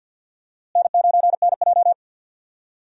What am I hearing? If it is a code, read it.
Morse code, heard as a keyed tone, about 53 wpm, 700 Hz. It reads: N9NJ
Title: USA Call Signs - 4 Characters - Repeated 2x - 50wpm